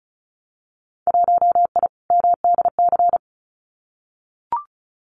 Morse code, heard as a keyed tone, about 36 wpm, 700 Hz. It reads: 1I MDC
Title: Winter Field Day Exchanges 35wpm